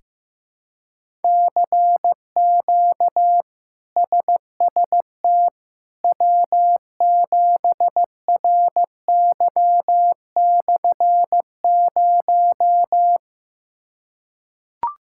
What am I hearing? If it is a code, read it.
Morse code, heard as a keyed tone, about 15 wpm, 700 Hz. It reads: CQ SST W7RY/0
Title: K1USN SST QSO Practice - 15wpm